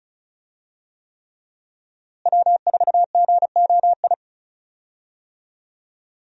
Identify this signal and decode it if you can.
Morse code, heard as a keyed tone, about 36 wpm, 700 Hz. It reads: W4GOI
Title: USA Call Signs - 5 Characters - Repeated 3x - 35wpm